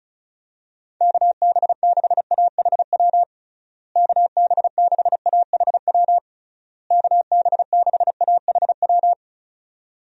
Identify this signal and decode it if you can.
Morse code, heard as a keyed tone, about 34 wpm, 700 Hz. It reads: KB6AHW KB6AHW KB6AHW
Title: USA Call Signs - 6 Characters - Repeated 3x - 35wpm